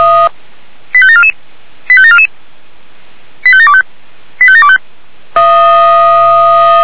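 Five-tone selective call (ZVEI), heard as quick runs of five tones, tones of about 80 ms. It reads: 85744 85744 85725 85725
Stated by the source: Five-tone ZVEI selective call received on SDR.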